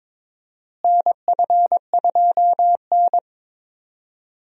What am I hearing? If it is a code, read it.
Morse code, heard as a keyed tone, about 22 wpm, 700 Hz. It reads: NF2N
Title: USA Call Signs - 4 Characters - Repeated 3x - 22wpm